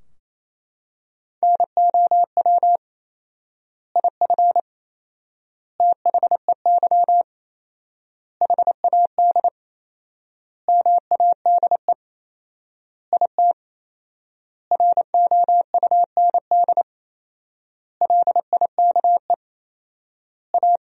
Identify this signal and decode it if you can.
Morse code, heard as a keyed tone, about 28 wpm, 700 Hz. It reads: NOW IF THEY HAD MADE IT ROUND LIKE A
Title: 4x Word-Spacing - Sentences from Top 500 Words - 28wpm